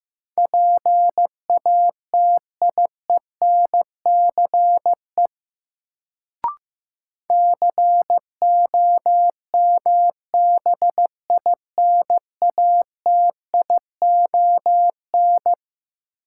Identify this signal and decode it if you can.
Morse code, heard as a keyed tone, about 15 wpm, 700 Hz. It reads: PATIENCE COMBINATION
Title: Sentences from Top 1300 Words - New Word Drill Alt - 15wpm